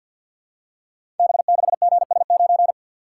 Morse code, heard as a keyed tone, about 51 wpm, 700 Hz. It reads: B6GS9
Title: Code Groups - 5 Letter:Numbers - 50wpm